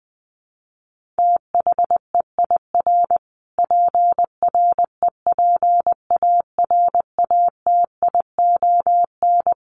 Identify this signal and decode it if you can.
Morse code, heard as a keyed tone, about 20 wpm, 700 Hz. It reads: THEIR PREPARATION
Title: Sentences from Top 1400 Words 20wpm